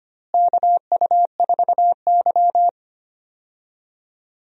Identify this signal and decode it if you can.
Morse code, heard as a keyed tone, about 25 wpm, 700 Hz. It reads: KU4Y
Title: USA Call Signs - 4 Characters - Repeated 2x - 25wpm